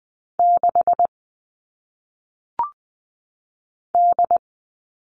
Morse code, heard as a keyed tone, about 20 wpm, 700 Hz. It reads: 6 D
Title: All Cut Numbers - 1-Digit 20wpm